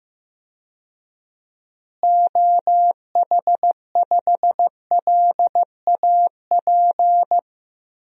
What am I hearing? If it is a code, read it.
Morse code, heard as a keyed tone, about 15 wpm, 700 Hz. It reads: OH5LAP
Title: DXCC-100 Call Signs - 6 Characters - Repeated 3x - 15wpm